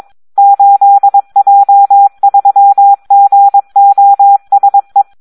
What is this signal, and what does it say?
Morse code, heard as a keyed tone, about 22 wpm, 800 Hz. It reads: 8J3GOSE
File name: Audio practica CW / SD 1 22